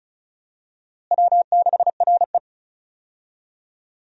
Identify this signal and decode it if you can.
Morse code, heard as a keyed tone, about 34 wpm, 700 Hz. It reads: W6RE
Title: USA Call Signs - 4 Characters - Repeated 3x - 35wpm